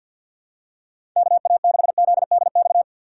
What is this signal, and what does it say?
Morse code, heard as a keyed tone, about 52 wpm, 700 Hz. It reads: KA6ZDX
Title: USA Call Signs - 6 Characters - 4x Character Spacing - 50wpm